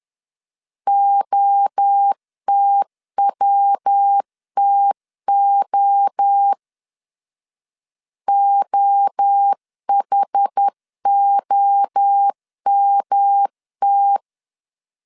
Morse code, heard as a keyed tone, about 10 wpm, 800 Hz. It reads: OTWTO OHOMT
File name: CW / DET / 1 3 1500C S A 10wpm 3447